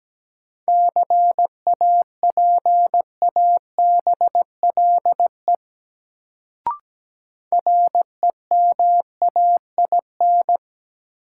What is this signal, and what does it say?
Morse code, heard as a keyed tone, about 17 wpm, 700 Hz. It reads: CAPABLE REMAIN
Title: Sentences from Top 1200 Words - New Word Drill - 17wpm